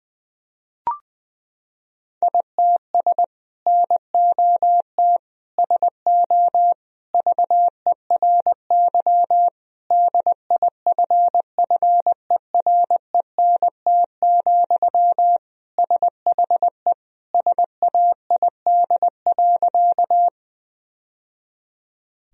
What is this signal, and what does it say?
Morse code, heard as a keyed tone, about 20 wpm, 700 Hz. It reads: ITS NOT SO VERY DIFFERENT, SHE SAID.
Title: Sentences from Top 1000 Words - Review Part 3 - 20wpm